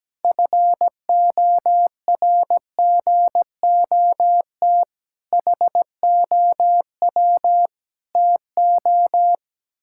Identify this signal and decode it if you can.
Morse code, heard as a keyed tone, about 17 wpm, 700 Hz. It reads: FORGOT HOW TO
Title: Pangram Challenge v2 - 17wpm